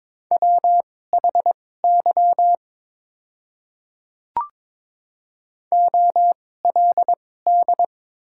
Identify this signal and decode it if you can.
Morse code, heard as a keyed tone, about 23 wpm, spaced as for 16 wpm, 700 Hz. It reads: WHY OLD
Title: 2x Character Spacing - Top 500 Words - 22wpm